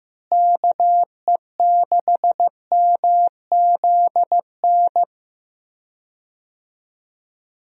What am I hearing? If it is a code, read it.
Morse code, heard as a keyed tone, about 15 wpm, 700 Hz. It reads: KE6MZN
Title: USA Call Signs - 6 Characters - Repeated 3x - 15wpm